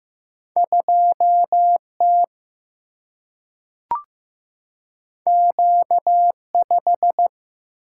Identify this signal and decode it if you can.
Morse code, heard as a keyed tone, about 15 wpm, 700 Hz. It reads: 2T Q5
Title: Single-Letter Single-Number 15wpm